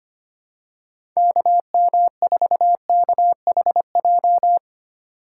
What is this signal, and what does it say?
Morse code, heard as a keyed tone, about 25 wpm, 700 Hz. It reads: KM4KHJ
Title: USA Call Signs - 6 Characters - 2x Character Spacing - 25wpm